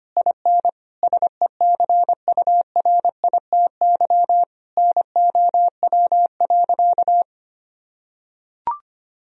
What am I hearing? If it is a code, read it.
Morse code, heard as a keyed tone, about 25 wpm, 700 Hz. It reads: IN SECURITY NOW.
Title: Sentences from Top 1000 Words - Review Part 4 - 25wpm